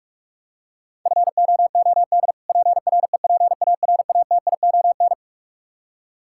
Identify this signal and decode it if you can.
Morse code, heard as a keyed tone, about 46 wpm, 700 Hz. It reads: FOOD PREPARATION
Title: Practice QSO - Exchange 3 - More Info - 45wpm